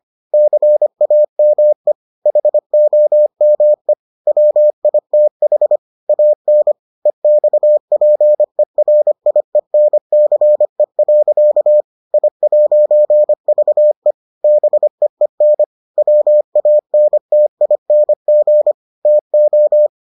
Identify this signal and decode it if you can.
Morse code, heard as a keyed tone, about 25 wpm, 600 Hz. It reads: CAME HOME WITH AN EXPERIENCE. I'VE BEEN WANTING TO